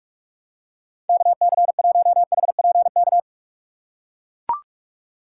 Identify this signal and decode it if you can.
Morse code, heard as a keyed tone, about 44 wpm, 700 Hz. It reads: KC1HPK
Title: USA Call Signs - 6 Characters - 6x Character Spacing - 45wpm